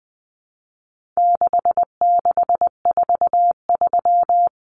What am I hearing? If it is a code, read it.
Morse code, heard as a keyed tone, about 20 wpm, 700 Hz. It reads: 6643
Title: Common Cut Numbers - 4-Digit 20wpm